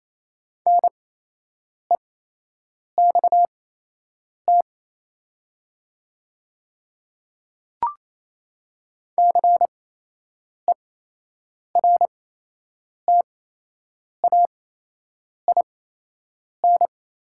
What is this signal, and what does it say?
Morse code, heard as a keyed tone, about 28 wpm, spaced as for 8 wpm, 700 Hz. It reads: NEXT CERTAIN
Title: 8x Character Spacing - Top 500 Words - 28wpm